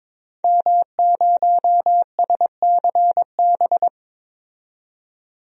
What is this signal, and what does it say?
Morse code, heard as a keyed tone, about 23 wpm, 700 Hz. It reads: M0SCB
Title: DXCC-100 Call Signs - 5 Characters - 22wpm